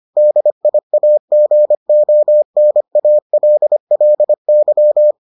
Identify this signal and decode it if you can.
Morse code, heard as a keyed tone, about 25 wpm, 600 Hz. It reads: DIAGONALLY